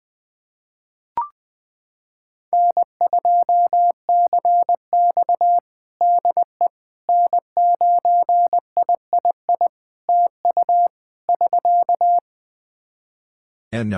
Morse code, heard as a keyed tone, about 21 wpm, 700 Hz. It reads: N2CX DE N9III TU <SK>
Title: K1USN SST QSO Practice 20wpm